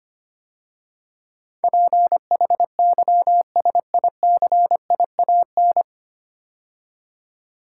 Morse code, heard as a keyed tone, about 25 wpm, 700 Hz. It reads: PHYSICIAN
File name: practice / 25wpm / 9 Letter Words - First 3 Letters Given - 25wpm